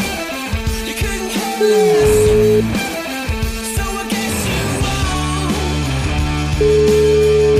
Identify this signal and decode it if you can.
Ringback tone (Europe, 425 Hz), with noise.